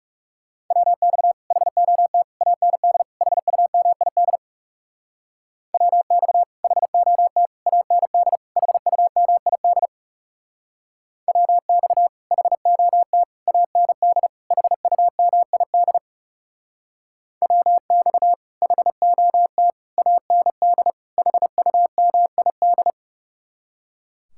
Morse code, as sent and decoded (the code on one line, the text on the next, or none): .-- -..- / .... --- - / .- -. -.. / .... ..- -- .. -.. / .-- -..- / .... --- - / .- -. -.. / .... ..- -- .. -.. / .-- -..- / .... --- - / .- -. -.. / .... ..- -- .. -.. / .-- -..- / .... --- - / .- -. -.. / .... ..- -- .. -..
WX HOT AND HUMID WX HOT AND HUMID WX HOT AND HUMID WX HOT AND HUMID